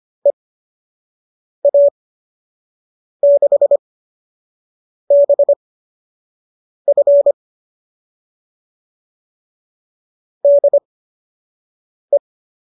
Morse code, heard as a keyed tone, about 25 wpm, spaced as for 6 wpm, 570 Hz. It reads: EA6BF DE